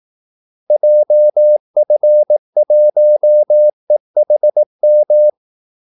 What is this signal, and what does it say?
Morse code, heard as a keyed tone, about 18 wpm, 600 Hz. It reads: JF1EHM
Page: Kaldesignaler 18 wpm